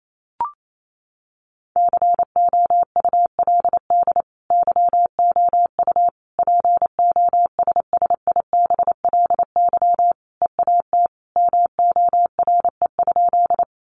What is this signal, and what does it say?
Morse code, heard as a keyed tone, about 28 wpm, 700 Hz. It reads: COULD YOU POSSIBLY EAT MORE?
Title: Sentences from Top 1300 Words 28wpm